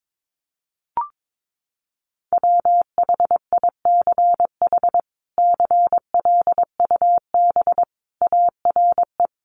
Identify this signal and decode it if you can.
Morse code, heard as a keyed tone, about 22 wpm, 700 Hz. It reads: WHICH CLUB ARE
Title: Sentences from Top 1200 Words 22wpm